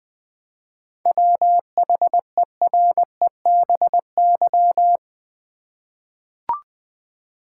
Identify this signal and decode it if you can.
Morse code, heard as a keyed tone, about 20 wpm, 700 Hz. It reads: WHEREBY
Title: 7 Letter Words - First 3 Letters Given - 20wpm